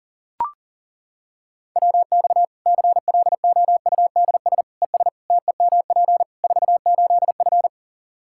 Morse code, heard as a keyed tone, about 40 wpm, 700 Hz. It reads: WX CLOUDS ES TEMP 48F
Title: Practice QSO - Exchange 2 - Info - 40wpm